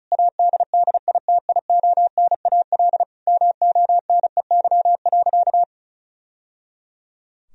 Morse code, heard as a keyed tone, about 36 wpm, 700 Hz. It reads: ADDITIONAL MONEY.
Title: Sentences from Top 1000 Words - Review Part 4 - 35wpm